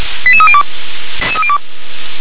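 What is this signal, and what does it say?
Five-tone selective call (ZVEI), heard as quick runs of five tones, tones of about 70 ms. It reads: 88302
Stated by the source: ZVEI five-tone selective call received on an SDR.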